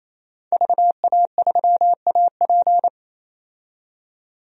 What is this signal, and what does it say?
Morse code, heard as a keyed tone, about 28 wpm, 700 Hz. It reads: VA3AP
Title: DXCC-100 Call Signs - 5 Characters - 28wpm